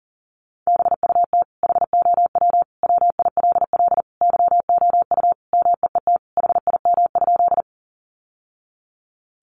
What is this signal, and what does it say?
Morse code, heard as a keyed tone, about 40 wpm, 700 Hz. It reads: BUT HOW WILL YOU MEET HIM?
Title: Sentences from Top 1100 Words 40wpm